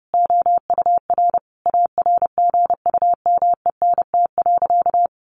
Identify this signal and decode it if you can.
Morse code, heard as a keyed tone, about 30 wpm, 700 Hz. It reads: OUR ARGUMENT.
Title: Sentences from Top 1500 Words 30wpm